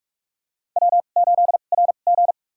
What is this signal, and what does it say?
Morse code, heard as a keyed tone, about 44 wpm, spaced as for 32 wpm, 690 Hz. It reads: W8RG
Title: USA Call Signs - 4 Characters - 2x Character Spacing - 45wpm